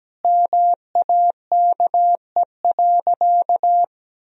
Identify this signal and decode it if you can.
Morse code, heard as a keyed tone, about 17 wpm, 700 Hz. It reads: MAKE.
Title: Sentences from Top 1000 Words - Review Part 1 - 17wpm